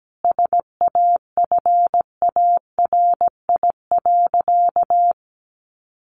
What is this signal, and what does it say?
Morse code, heard as a keyed tone, about 17 wpm, 700 Hz. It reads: SAFARI.
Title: A Sound Of Thunder 17wpm